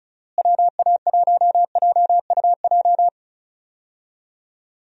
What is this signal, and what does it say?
Morse code, heard as a keyed tone, about 34 wpm, 700 Hz. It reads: WA1JUJ
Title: USA Call Signs - 6 Characters - Repeated 3x - 35wpm